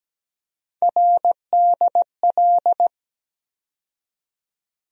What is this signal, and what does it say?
Morse code, heard as a keyed tone, about 17 wpm, 700 Hz. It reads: RDL
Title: USA Call Sign Postfixes - 3 Letters 17wpm